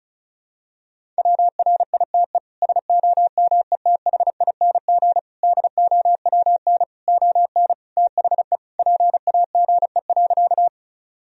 Morse code, heard as a keyed tone, about 36 wpm, 700 Hz. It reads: WRITE SOMETHING DOWN ON THE PAGE.
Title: Sentences from Top 1000 Words - Review Part 1 - 35wpm